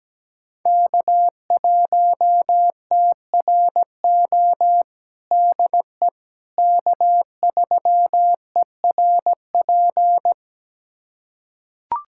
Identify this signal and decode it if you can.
Morse code, heard as a keyed tone, about 17 wpm, 700 Hz. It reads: K1TRO DE K3ERP
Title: USA Callsign DE Callsign 17wpm